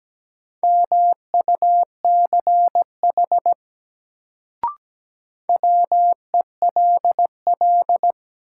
Morse code, heard as a keyed tone, about 17 wpm, 700 Hz. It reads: MUCH WELL
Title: Sets of 2 Words - New Word Drill - 17wpm